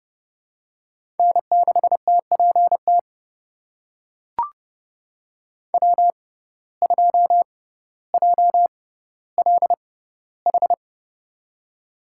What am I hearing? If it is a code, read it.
Morse code, heard as a keyed tone, about 30 wpm, spaced as for 15 wpm, 700 Hz. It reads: N6TPT W2JLH
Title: USA Call Signs - 5 Characters - 6x Character Spacing - 30wpm